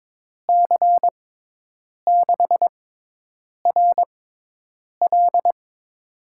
Morse code, heard as a keyed tone, about 22 wpm, spaced as for 8 wpm, 700 Hz. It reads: C6RL
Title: USA Call Signs - 5 Characters - 6x Character Spacing - 22wpm